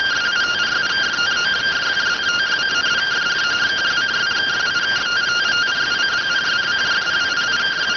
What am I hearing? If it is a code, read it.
Two-tone teleprinter signal, about 170 Hz shift.